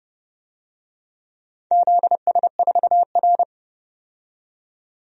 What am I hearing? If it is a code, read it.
Morse code, heard as a keyed tone, about 30 wpm, 700 Hz. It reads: ZS4R
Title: DXCC-100 Call Signs - 4 Characters - 30wpm